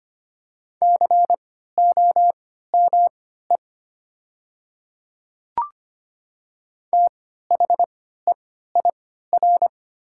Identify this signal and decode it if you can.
Morse code, heard as a keyed tone, about 25 wpm, spaced as for 14 wpm, 700 Hz. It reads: COME THEIR
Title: 3x Character Spacing - Top 100 Words - 25wpm